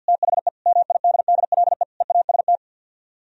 Morse code, heard as a keyed tone, about 62 wpm, 700 Hz. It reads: THE MIDDLE EAST
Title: Speed-Racing - Sets of 4 Words 50wpm